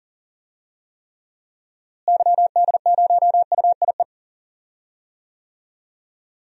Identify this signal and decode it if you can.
Morse code, heard as a keyed tone, about 38 wpm, 700 Hz. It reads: YD0UIE
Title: DXCC-100 Call Signs - 6 Characters - Repeated 2x - 40wpm